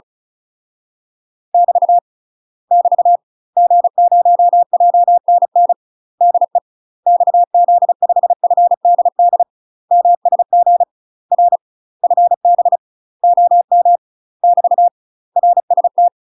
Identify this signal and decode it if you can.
Morse code, heard as a keyed tone, about 35 wpm, 700 Hz. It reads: <BT> <BT> G0JNN DE XZ5FDD MSG R FB OM <BT> RST
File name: UK MASTER QSOs TRAINING / 35WPM / UK MASTER QSOs 0007